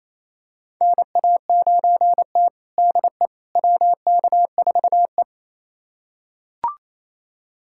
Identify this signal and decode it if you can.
Morse code, heard as a keyed tone, about 28 wpm, 700 Hz. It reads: NA9T DE WK4E
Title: USA Callsign DE Callsign - 4 Characters - 28wpm